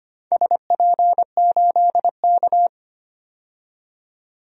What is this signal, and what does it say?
Morse code, heard as a keyed tone, about 25 wpm, 700 Hz. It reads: SP8K
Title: DXCC-100 Call Signs - 4 Characters - 25wpm